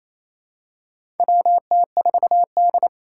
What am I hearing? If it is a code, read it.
Morse code, heard as a keyed tone, about 28 wpm, 700 Hz. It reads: WT4D
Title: USA Call Signs - 4 Characters - 3x Character Spacing - 28wpm